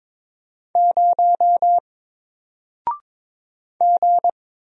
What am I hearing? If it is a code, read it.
Morse code, heard as a keyed tone, about 21 wpm, 700 Hz. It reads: 0 G
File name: Single-Letter-Number-22wpm